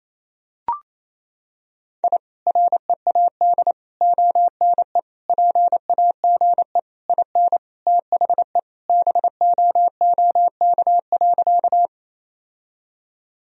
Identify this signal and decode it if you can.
Morse code, heard as a keyed tone, about 27 wpm, 700 Hz. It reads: I READ ONE PAGE IN THE BOOK.
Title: Sentences from Top 1000 Words - Review Part 1 - 28wpm